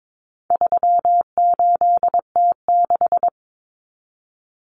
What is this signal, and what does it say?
Morse code, heard as a keyed tone, about 22 wpm, 700 Hz. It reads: 38T6
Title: Common Cut Numbers - 4-Digit 22wpm